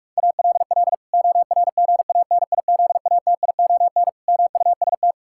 Morse code, heard as a keyed tone, about 45 wpm, 700 Hz. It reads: ALL ORGANIZATION MUST
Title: Sentences from Top 1000 Words - Review Part 5 - 45wpm